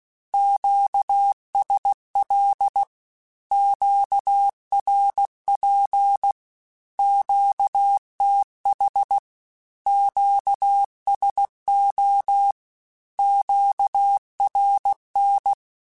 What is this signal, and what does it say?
Morse code, heard as a keyed tone, about 16 wpm, 800 Hz. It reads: QSL QRP QTH QSO QRN
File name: Q Calls / Q Calls-mp3s / Q Calls Once @15